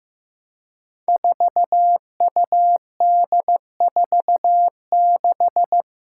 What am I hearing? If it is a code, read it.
Morse code, heard as a keyed tone, about 15 wpm, 700 Hz. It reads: 4UD46
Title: Five Challenging Letter-Number Combinations 15wpm